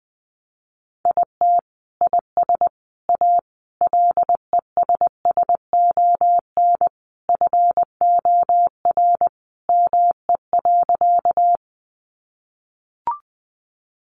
Morse code, heard as a keyed tone, about 20 wpm, 700 Hz. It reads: IT IS A LESSON FOR ME.
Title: Sentences from Top 1200 Words 20wpm